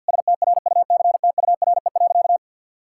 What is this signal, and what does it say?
Morse code, heard as a keyed tone, about 52 wpm, 700 Hz. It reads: STRUCTURE.
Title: Sentences from Top 1000 Words - Review Part 6 - 50wpm